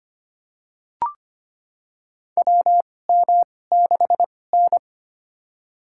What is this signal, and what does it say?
Morse code, heard as a keyed tone, about 25 wpm, spaced as for 18 wpm, 700 Hz. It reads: WM6N
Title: USA Call Signs - 4 Characters - 2x Character Spacing - 25wpm